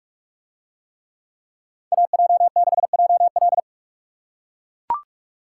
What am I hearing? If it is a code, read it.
Morse code, heard as a keyed tone, about 44 wpm, 700 Hz. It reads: AJ6JL